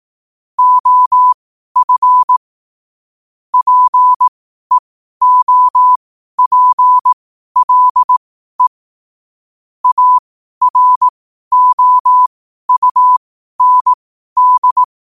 Morse code, heard as a keyed tone, about 18 wpm, spaced as for 12 wpm, 1000 Hz. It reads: OF PEOPLE AROUND
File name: qotdfiles / 12wpm